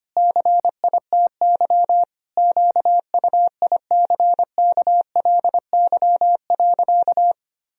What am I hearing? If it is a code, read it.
Morse code, heard as a keyed tone, about 25 wpm, 700 Hz. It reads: CITY QUICKLY.
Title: Sentences from Top 1000 Words - Review Part 1 - 25wpm